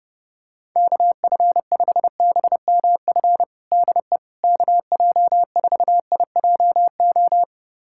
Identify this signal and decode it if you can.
Morse code, heard as a keyed tone, about 30 wpm, 700 Hz. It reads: KF5BMF DE KJ4IJO
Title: USA Callsign DE Callsign - 6 Characters - 30wpm